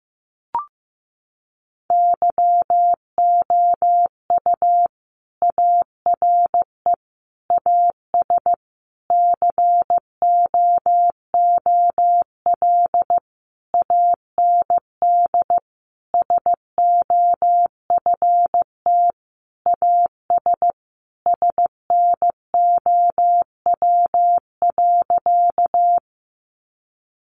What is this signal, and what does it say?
Morse code, heard as a keyed tone, about 15 wpm, 700 Hz. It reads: YOU ARE AS COOL AND SOFT AS SNOW.
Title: Sentences from Top 1500 Words 15wpm